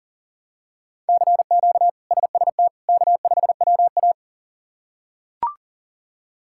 Morse code, heard as a keyed tone, about 40 wpm, 700 Hz. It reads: CQ SST K5WA
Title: K1USN SST QSO Practice - 40wpm